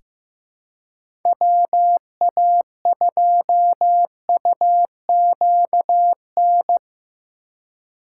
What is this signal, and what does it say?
Morse code, heard as a keyed tone, about 15 wpm, 700 Hz. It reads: WA2UQN
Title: USA Call Signs - 6 Characters - 4x Character Spacing - 15wpm